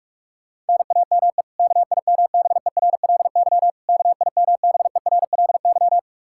Morse code, heard as a keyed tone, about 46 wpm, 700 Hz. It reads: NAME KIMBERLY KIMBERLY
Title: Practice QSO - Exchange 1 - Beginning - 45wpm